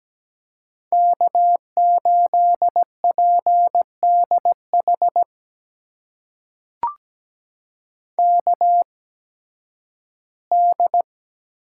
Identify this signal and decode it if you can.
Morse code, heard as a keyed tone, about 17 wpm, spaced as for 11 wpm, 700 Hz. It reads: K8PDH KD
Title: USA Call Signs - 8x Character Spacing - 17wpm